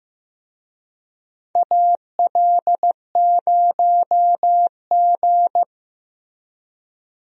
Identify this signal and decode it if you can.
Morse code, heard as a keyed tone, about 16 wpm, 700 Hz. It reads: AL0G
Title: USA Call Signs - 4 Characters - Repeated 2x - 15wpm